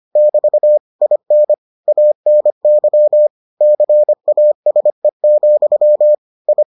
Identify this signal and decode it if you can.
Morse code, heard as a keyed tone, about 25 wpm, 600 Hz. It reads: <BT> IN ANY CASE, I